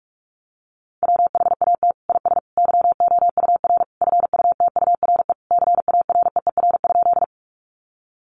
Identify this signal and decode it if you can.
Morse code, heard as a keyed tone, about 44 wpm, 700 Hz. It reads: WHAT IS YOUR FUTURE CAREER?
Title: Sentences from Top 1000 Words 45wpm